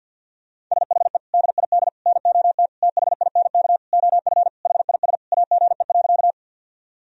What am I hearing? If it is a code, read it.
Morse code, heard as a keyed tone, about 52 wpm, 700 Hz. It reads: SHE DID NOT THINK OF HIS AGE.
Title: Sentences from Top 1000 Words - Review Part 3 - 50wpm